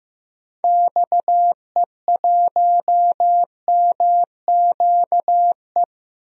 Morse code, heard as a keyed tone, about 15 wpm, 700 Hz. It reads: XE1MQE
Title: DXCC-100 Call Signs - 6 Characters - Repeated 3x - 15wpm